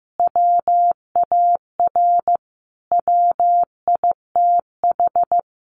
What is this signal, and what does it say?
Morse code, heard as a keyed tone, about 15 wpm, 700 Hz. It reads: WAR WITH
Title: Sentences from Top 1300 Words 15wpm